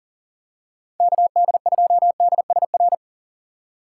Morse code, heard as a keyed tone, about 40 wpm, 700 Hz. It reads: KD2DSR
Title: USA Call Signs - 6x Character Spacing - 40wpm